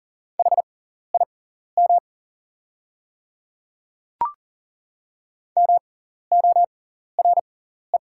Morse code, heard as a keyed tone, about 39 wpm, spaced as for 14 wpm, 690 Hz. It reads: HIM MORE
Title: 6x Character Spacing - Top 100 Words - 40wpm